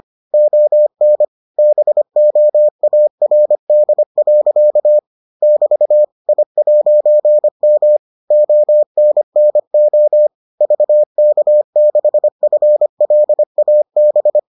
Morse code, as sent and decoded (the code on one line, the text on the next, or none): --- -. / -... --- .- .-. -.. .-.-.- / -...- / .. .----. -- / --- -. -. --- / ...- -.- -.... ..-. .-.. .- -...
ON BOARD. <BT> I'M ONNO VK6FLAB